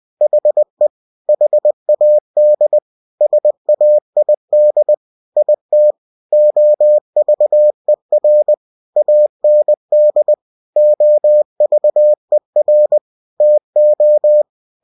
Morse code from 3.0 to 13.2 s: SAID IT OVER AND OVER